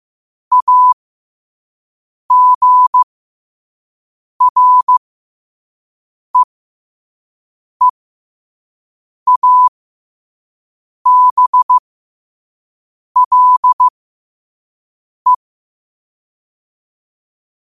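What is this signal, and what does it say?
Morse code, heard as a keyed tone, about 15 wpm, spaced as for 5 wpm, 1000 Hz. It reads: AGREEABLE
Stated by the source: Morse at 5 wpm.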